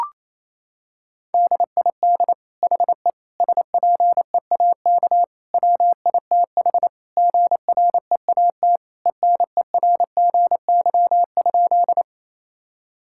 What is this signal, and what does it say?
Morse code, heard as a keyed tone, about 28 wpm, 700 Hz. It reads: DID HE SPEAK WITH GREAT ENERGY?
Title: Sentences from Top 1000 Words - Review Part 6 - 28wpm